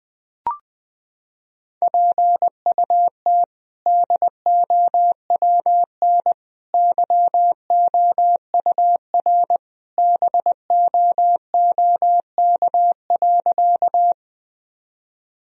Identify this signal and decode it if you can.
Morse code, heard as a keyed tone, about 20 wpm, 700 Hz. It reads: PUT DOWN YOUR BOOK.
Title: Sentences from Top 1000 Words - Review Part 1 - 20wpm